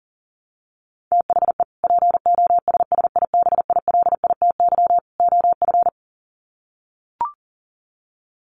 Morse code, heard as a keyed tone, about 40 wpm, 700 Hz. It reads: THE POSSIBILITY OF
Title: Sets of 3 Words 40wpm